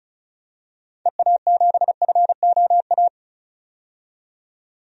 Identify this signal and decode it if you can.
Morse code, heard as a keyed tone, about 36 wpm, 700 Hz. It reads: EA7FOA
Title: DXCC-100 Call Signs - 6 Characters - 35wpm